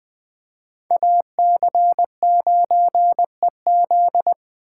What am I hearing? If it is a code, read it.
Morse code, heard as a keyed tone, about 20 wpm, 700 Hz. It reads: AC9EZ
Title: Speed Racing - USA Call Signs 17wpm